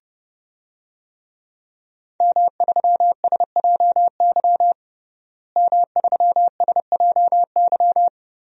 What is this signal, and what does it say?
Morse code, heard as a keyed tone, about 32 wpm, 700 Hz. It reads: M3SJY M3SJY
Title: DXCC-100 Call Signs - 5 Characters - Repeated 3x - 30wpm